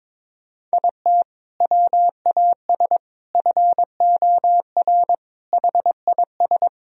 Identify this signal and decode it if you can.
Morse code, heard as a keyed tone, about 21 wpm, 700 Hz. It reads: IT WAS FOR HIS
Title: Sentences from Top 1000 Words - Review Part 3 - 22wpm